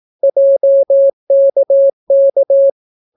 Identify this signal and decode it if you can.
Morse code, heard as a keyed tone, about 18 wpm, 550 Hz. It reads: JKK